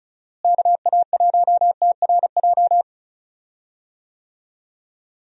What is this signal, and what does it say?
Morse code, heard as a keyed tone, about 34 wpm, 700 Hz. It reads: KA1TRJ
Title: USA Call Signs - 6 Characters - Repeated 2x - 35wpm